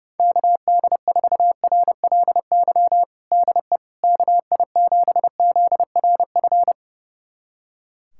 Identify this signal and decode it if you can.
Morse code, heard as a keyed tone, about 30 wpm, 700 Hz. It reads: KD4RLY DE KI7ZRF
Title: USA Callsign DE Callsign 30wpm